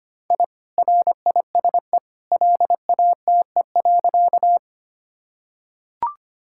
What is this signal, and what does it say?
Morse code, heard as a keyed tone, about 25 wpm, 700 Hz. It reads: I RISE LATE.
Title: Sentences from Top 1000 Words - Review Part 6 - 25wpm